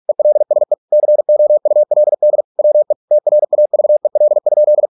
Morse code, heard as a keyed tone, about 46 wpm, 600 Hz. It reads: ELSE COULD WE TRAVEL?